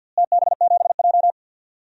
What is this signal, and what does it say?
Morse code, heard as a keyed tone, about 49 wpm, 700 Hz. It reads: TB7J
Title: DXCC-100 Call Signs - 4 Characters - Repeated 2x - 50wpm